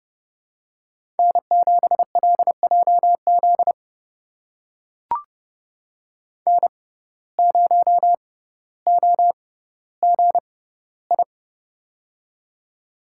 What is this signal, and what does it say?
Morse code, heard as a keyed tone, about 31 wpm, spaced as for 16 wpm, 700 Hz. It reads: N7LJZ N0OGI
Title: USA Call Signs - 5 Characters - 6x Character Spacing - 30wpm